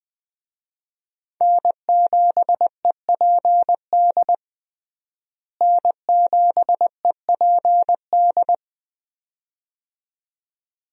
Morse code, heard as a keyed tone, about 21 wpm, 700 Hz. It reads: N7EPD N7EPD
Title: USA Call Signs - 5 Characters - Repeated 2x - 20wpm